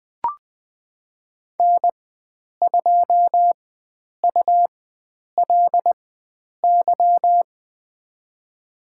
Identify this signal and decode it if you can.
Morse code, heard as a keyed tone, about 20 wpm, spaced as for 9 wpm, 700 Hz. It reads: N2ULY
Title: USA Call Signs - 5 Characters - 4x Character Spacing - 20wpm